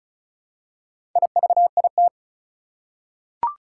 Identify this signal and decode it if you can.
Morse code, heard as a keyed tone, about 34 wpm, 700 Hz. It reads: IVIT
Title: Four Challenging Letter-Number Combinations 35wpm